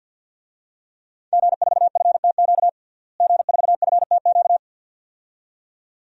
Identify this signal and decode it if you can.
Morse code, heard as a keyed tone, about 50 wpm, 700 Hz. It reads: G4FTQ G4FTQ
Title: DXCC-100 Call Signs - 5 Characters - Repeated 2x - 50wpm